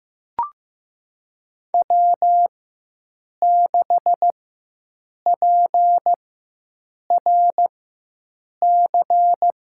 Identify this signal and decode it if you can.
Morse code, heard as a keyed tone, about 15 wpm, spaced as for 7 wpm, 700 Hz. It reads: W6PRC
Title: USA Call Signs - 4x Character Spacing - 15wpm